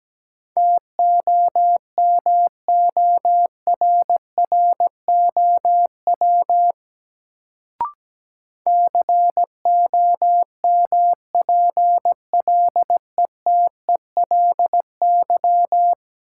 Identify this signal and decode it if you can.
Morse code, heard as a keyed tone, about 17 wpm, 700 Hz. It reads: TOMORROW COMPLETELY
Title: Sentences from Top 1100 Words - New Word Drill - 17wpm